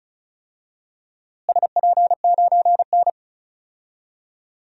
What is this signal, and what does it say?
Morse code, heard as a keyed tone, about 36 wpm, 700 Hz. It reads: SP9N
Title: DXCC-100 Call Signs - 4 Characters - Repeated 2x - 35wpm